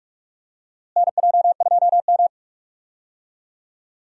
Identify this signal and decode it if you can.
Morse code, heard as a keyed tone, about 46 wpm, 700 Hz. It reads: NJ2M
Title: USA Call Signs - 4 Characters - Repeated 2x - 45wpm